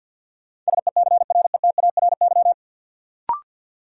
Morse code, heard as a keyed tone, about 53 wpm, 700 Hz. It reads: SECRETARY
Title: Sentences from Top 1300 Words - New Word Drill - 50wpm